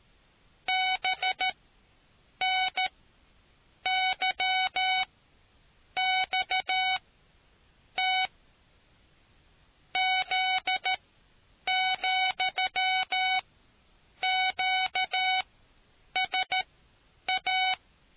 Morse code, heard as a keyed tone, about 13 wpm, spaced as for 7 wpm, 750 Hz. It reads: BNYXT Z,QSA